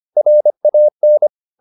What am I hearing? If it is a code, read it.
Morse code, heard as a keyed tone, about 25 wpm, 600 Hz. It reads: RAN
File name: the old man and the sea part 1 25wpm